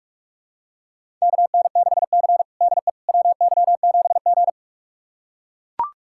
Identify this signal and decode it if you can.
Morse code, heard as a keyed tone, about 41 wpm, 700 Hz. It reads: KN6C DE WY7G
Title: USA Callsign DE Callsign - 4 Characters - 45wpm